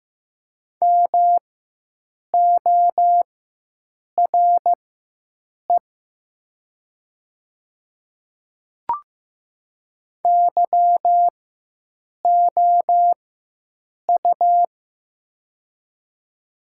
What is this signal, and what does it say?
Morse code, heard as a keyed tone, about 15 wpm, spaced as for 7 wpm, 700 Hz. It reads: MORE YOU
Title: 4x Character Spacing - Top 100 Words - 15wpm